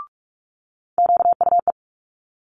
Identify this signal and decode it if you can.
Morse code, heard as a keyed tone, about 46 wpm, 700 Hz. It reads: QUE
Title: Multi-letter Phonemes Quick Fire 45wpm